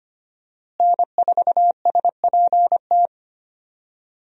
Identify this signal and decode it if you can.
Morse code, heard as a keyed tone, about 25 wpm, 700 Hz. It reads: N4SPT
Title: USA Call Signs - 5 Characters - Repeated 2x - 25wpm